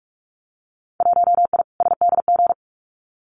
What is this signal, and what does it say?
Morse code, heard as a keyed tone, about 46 wpm, 700 Hz. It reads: 1I SDG
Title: Winter Field Day Exchanges 45wpm